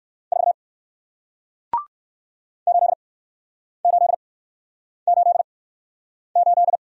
Morse code, heard as a keyed tone, about 60 wpm, spaced as for 9 wpm, 700 Hz. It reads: 4 8888